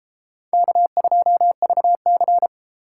Morse code, heard as a keyed tone, about 28 wpm, 700 Hz. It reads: K2VC